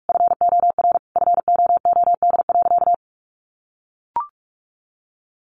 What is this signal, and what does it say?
Morse code, heard as a keyed tone, about 46 wpm, 700 Hz. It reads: FOR FOOD.
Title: A Sound Of Thunder 45wpm